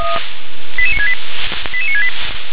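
Five-tone selective call (ZVEI), heard as quick runs of five tones, tones of about 70 ms. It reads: 88968 88968
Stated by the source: Five-tone ZVEI selective call received on an SDR.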